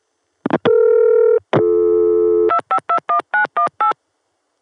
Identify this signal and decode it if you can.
DTMF, heard as telephone touch tones, tones of about 100 ms. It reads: 2521918